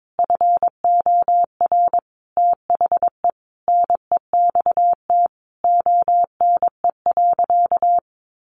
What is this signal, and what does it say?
Morse code, heard as a keyed tone, about 22 wpm, 700 Hz. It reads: FOR THE NEXT ONE.
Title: Sentences from Top 400 Words 22wpm